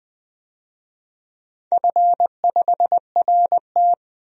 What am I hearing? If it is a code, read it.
Morse code, heard as a keyed tone, about 20 wpm, 700 Hz. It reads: F5RT
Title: DXCC-100 Call Signs - 4 Characters - 20wpm